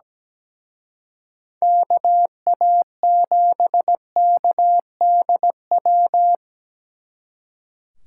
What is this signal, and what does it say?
Morse code, heard as a keyed tone, about 20 wpm, 700 Hz. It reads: KA7KDW
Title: Speed Racing - USA 2x3 Call Signs 17wpm